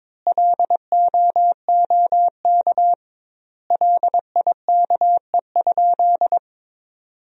Morse code, heard as a keyed tone, about 22 wpm, 700 Hz. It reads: LOOK LIKE?
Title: 2x Word-Spacing - Sentences from Top 100 Words - 22wpm